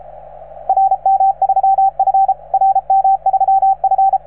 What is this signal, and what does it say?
Morse code, heard as a keyed tone, about 33 wpm, 770 Hz, with radio noise. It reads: RM3F RM3F